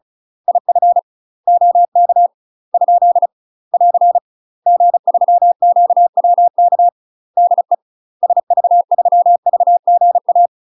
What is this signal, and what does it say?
Morse code, heard as a keyed tone, about 36 wpm, 700 Hz. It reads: IF OK ? <AR> G3QWK DE SV3VGA
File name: UK MASTER QSOs TRAINING / 35WPM / UK MASTER QSOs 0035